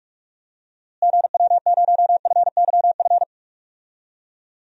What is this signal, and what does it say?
Morse code, heard as a keyed tone, about 44 wpm, 700 Hz. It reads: GW0FYF
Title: DXCC-100 Call Signs - 6 Characters - Repeated 3x - 45wpm